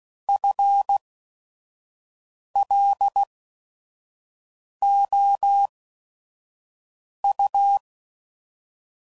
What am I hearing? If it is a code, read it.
Morse code, heard as a keyed tone, about 16 wpm, spaced as for 5 wpm, 800 Hz. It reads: FLOU